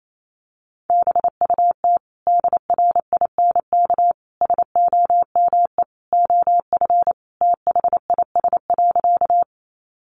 Morse code, heard as a keyed tone, about 28 wpm, 700 Hz. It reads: BUT DRINK SOME OF THIS.
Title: Sentences from Top 1500 Words 28wpm